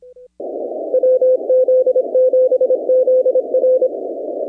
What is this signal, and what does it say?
Morse code, heard as a keyed tone, about 24 wpm, 500 Hz. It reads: WZ7ZR
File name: contests / 2019-cw-wpx-kt5j-k5pi- -2330 0-15